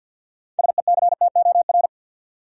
Morse code, heard as a keyed tone, about 52 wpm, 700 Hz. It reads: SECTOR
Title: Six Letter Words - 50wpm